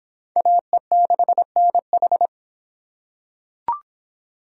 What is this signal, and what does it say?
Morse code, heard as a keyed tone, about 26 wpm, 700 Hz. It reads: AE6NH